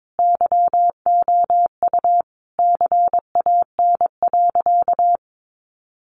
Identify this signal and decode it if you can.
Morse code, heard as a keyed tone, about 22 wpm, 700 Hz. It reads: YOU CAN.
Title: Sentences from Top 1400 Words 22wpm